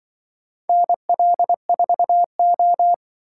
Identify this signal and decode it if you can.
Morse code, heard as a keyed tone, about 24 wpm, 700 Hz. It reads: NL4O
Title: Speed-Racing - Four Letter-Number Combinations - 20wpm